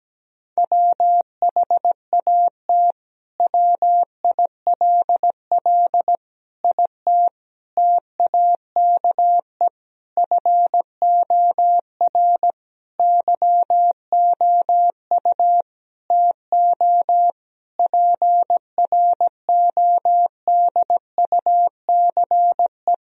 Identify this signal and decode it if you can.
Morse code, heard as a keyed tone, about 17 wpm, 700 Hz. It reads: WHAT WILL IT TAKE FOR YOU TO PRODUCE
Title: Sentences from Top 1000 Words - Review Part 3 - 17wpm